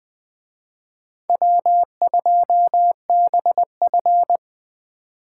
Morse code, heard as a keyed tone, about 21 wpm, 700 Hz. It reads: W2BF